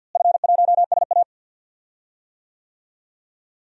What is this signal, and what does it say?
Morse code, heard as a keyed tone, about 52 wpm, 700 Hz. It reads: F1SA
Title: DXCC-100 Call Signs - 4 Characters - Repeated 3x - 50wpm